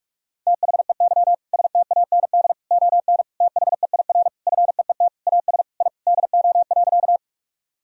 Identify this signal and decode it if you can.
Morse code, heard as a keyed tone, about 46 wpm, 700 Hz. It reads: THEY STAND ON THEIR FEET AS I DO.
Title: Sentences from Top 1000 Words - Review Part 1 - 45wpm